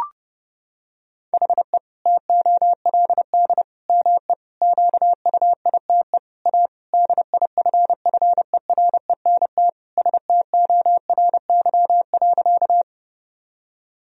Morse code, heard as a keyed tone, about 30 wpm, 700 Hz. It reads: HE TOLD ME QUITE A DIFFERENT STORY.
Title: Sentences from Top 1000 Words - Review Part 3 - 30wpm